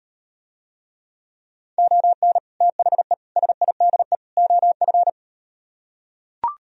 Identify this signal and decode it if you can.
Morse code, heard as a keyed tone, about 38 wpm, 700 Hz. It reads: ON THE SIDE OF